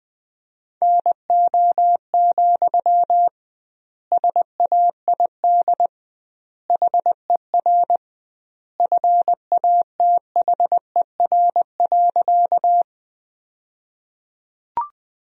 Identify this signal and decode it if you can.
Morse code, heard as a keyed tone, about 20 wpm, 700 Hz. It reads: NO, SAID HER FATHER.
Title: 2x Word-Spacing - Sentences from Top 300 Words - 20wpm